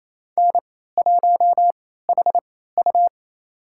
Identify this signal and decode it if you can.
Morse code, heard as a keyed tone, about 28 wpm, spaced as for 16 wpm, 700 Hz. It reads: N1HU
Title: USA Call Signs - 4 Characters - 3x Character Spacing - 28wpm